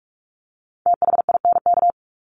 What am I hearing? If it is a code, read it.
Morse code, heard as a keyed tone, about 44 wpm, 700 Hz. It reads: THINK
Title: Top 300 Words 45wpm